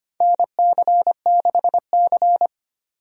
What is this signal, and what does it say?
Morse code, heard as a keyed tone, about 25 wpm, 700 Hz. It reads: NC6C
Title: USA Call Signs - 4 Characters - 25wpm